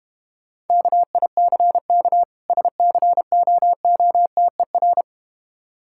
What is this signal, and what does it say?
Morse code, heard as a keyed tone, about 32 wpm, 700 Hz. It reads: KICK SCOOTER